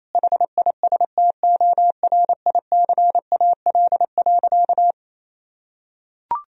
Morse code, heard as a keyed tone, about 28 wpm, 700 Hz. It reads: HISTORICAL.
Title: Sentences from Top 1000 Words - Review Part 4 - 28wpm